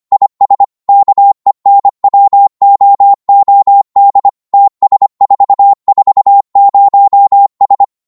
Morse code, heard as a keyed tone, about 25 wpm, 800 Hz. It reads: IS KENWOOD TS440S
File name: Morse / qso-25wpm